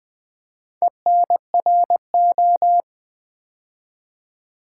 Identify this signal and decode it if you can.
Morse code, heard as a keyed tone, about 20 wpm, 690 Hz. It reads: ENRO
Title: Build A Word - 2 Letters At A Time - 6-Letter Words - 20wpm